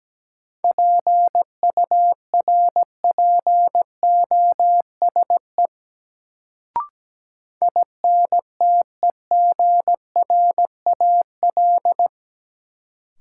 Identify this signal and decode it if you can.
Morse code, heard as a keyed tone, about 17 wpm, 700 Hz. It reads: PURPOSE INTEGRAL
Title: Sets of 5 Words - New Word Drill - 17wpm